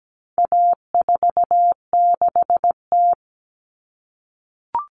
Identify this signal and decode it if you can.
Morse code, heard as a keyed tone, about 17 wpm, 700 Hz. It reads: A46T
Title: Common Cut Numbers - 4-Digit 17wpm